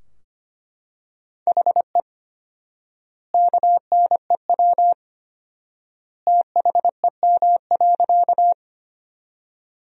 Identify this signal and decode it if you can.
Morse code, heard as a keyed tone, about 25 wpm, 700 Hz. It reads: HE KNEW THEM.
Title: 4x Word-Spacing - Sentences from Top 400 Words - 25wpm